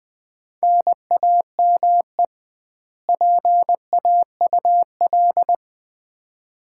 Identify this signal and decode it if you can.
Morse code, heard as a keyed tone, about 20 wpm, 700 Hz. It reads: NAME PAUL
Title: Common QSO Phrases - 2x Word-Spacing - 20wpm